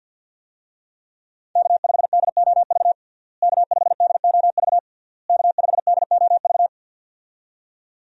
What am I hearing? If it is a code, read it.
Morse code, heard as a keyed tone, about 52 wpm, 700 Hz. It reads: K5DOV K5DOV K5DOV
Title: USA Call Signs - 5 Characters - Repeated 3x - 50wpm